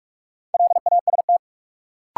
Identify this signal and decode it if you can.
Morse code, heard as a keyed tone, about 46 wpm, 700 Hz. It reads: LAST
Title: Sentences from Top 300 Words - New Word Drill - 45wpm